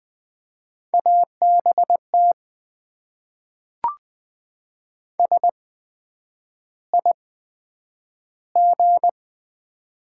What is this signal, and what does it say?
Morse code, heard as a keyed tone, about 20 wpm, spaced as for 9 wpm, 700 Hz. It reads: ABT SIG
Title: Top 90 QSO Elements - 8x Character Spacing - 20wpm